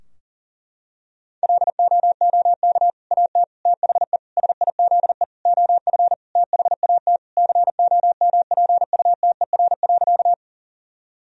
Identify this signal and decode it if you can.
Morse code, heard as a keyed tone, about 40 wpm, 700 Hz. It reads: LOOK AT THE SIZE OF THAT COMPUTER.
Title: Sentences from Top 1000 Words - Review Part 3 - 40wpm